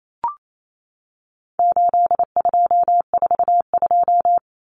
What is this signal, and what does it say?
Morse code, heard as a keyed tone, about 28 wpm, 700 Hz. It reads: 8242
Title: Common Cut Numbers - 4-Digit 28wpm